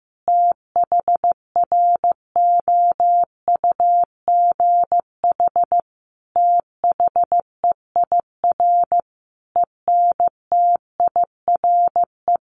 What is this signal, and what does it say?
Morse code, heard as a keyed tone, about 15 wpm, 700 Hz. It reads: THROUGH THEIR ENTIRE
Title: A Sound Of Thunder 15wpm